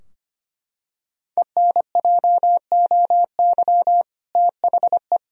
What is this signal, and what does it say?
Morse code, heard as a keyed tone, about 25 wpm, 700 Hz. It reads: ENJOY THE
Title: Sentences from Top 1000 Words - Review Part 4 - 25wpm